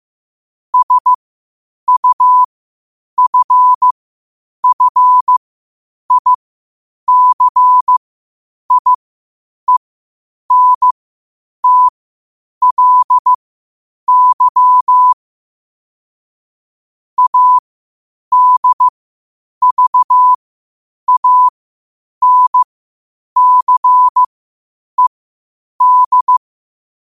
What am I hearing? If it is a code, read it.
Morse code, heard as a keyed tone, about 15 wpm, spaced as for 8 wpm, 1000 Hz. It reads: SUFFICIENTLY ADVANCED
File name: qotdfiles / 8wpm